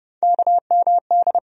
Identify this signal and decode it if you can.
Morse code, heard as a keyed tone, about 30 wpm, 700 Hz. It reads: KMD
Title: USA Call Sign Postfixes - 3 Letters 30wpm